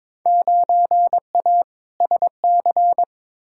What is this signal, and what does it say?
Morse code, heard as a keyed tone, about 21 wpm, 700 Hz. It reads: 9A SC
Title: Field Day Exchanges Encore - 22wpm